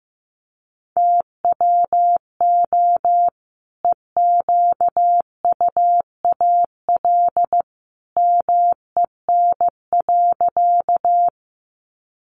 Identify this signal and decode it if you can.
Morse code, heard as a keyed tone, about 15 wpm, 700 Hz. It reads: TWO EQUAL MEN.
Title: Sentences from Top 1400 Words 15wpm